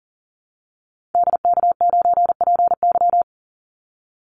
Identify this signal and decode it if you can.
Morse code, heard as a keyed tone, about 40 wpm, 700 Hz. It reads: DK9PY
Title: Call Signs - 40wpm